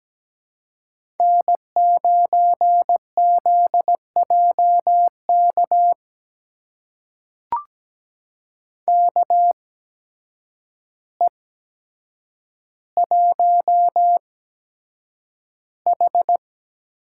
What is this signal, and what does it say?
Morse code, heard as a keyed tone, about 17 wpm, spaced as for 8 wpm, 700 Hz. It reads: N9ZJK KE1H
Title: USA Call Signs - 5 Characters - 8x Character Spacing - 17wpm